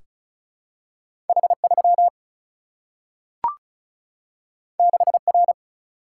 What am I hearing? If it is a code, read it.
Morse code, heard as a keyed tone, about 36 wpm, 700 Hz. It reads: H 3 6 R
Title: Single-Letter Single-Number 35wpm